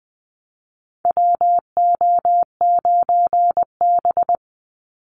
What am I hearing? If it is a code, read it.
Morse code, heard as a keyed tone, about 20 wpm, 700 Hz. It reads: WO9B
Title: Call Signs Repeated 3x - 20wpm